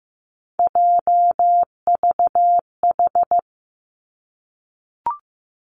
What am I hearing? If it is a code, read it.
Morse code, heard as a keyed tone, about 15 wpm, 700 Hz. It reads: JVH
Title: Three Challenging Letter-Number Combinations 15wpm